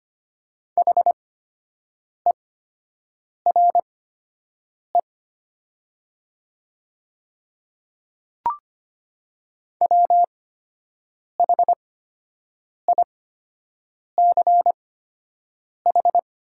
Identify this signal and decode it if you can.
Morse code, heard as a keyed tone, about 25 wpm, spaced as for 7 wpm, 700 Hz. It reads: HERE WHICH
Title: 8x Character Spacing - Top 300 Words - 25wpm